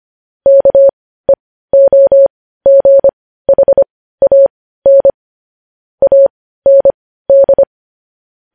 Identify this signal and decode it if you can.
Morse code, heard as a keyed tone, about 25 wpm, spaced as for 15 wpm, 550 Hz. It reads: KEOGHAN AND
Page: news-headlines-in-morse-code-at-15-wpm